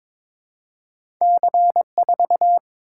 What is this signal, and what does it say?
Morse code, heard as a keyed tone, about 22 wpm, 700 Hz. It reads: C4
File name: practice / 22wpm / Two Letter-Number Combinations - 22wpm